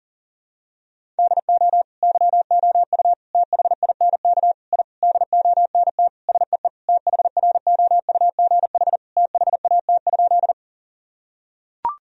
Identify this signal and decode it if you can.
Morse code, heard as a keyed tone, about 38 wpm, 700 Hz. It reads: DO YOU THINK I DONT SEE THROUGH THAT?
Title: Sentences from Top 1000 Words - Review Part 1 - 40wpm